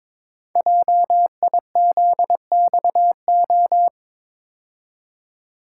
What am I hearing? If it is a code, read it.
Morse code, heard as a keyed tone, about 22 wpm, 700 Hz. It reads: JIZXO
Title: Code Groups - 5 Letters - 22wpm